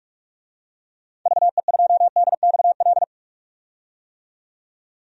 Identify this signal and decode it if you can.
Morse code, heard as a keyed tone, about 45 wpm, 700 Hz. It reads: VE2DXL
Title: DXCC-100 Call Signs - 6 Characters - 45wpm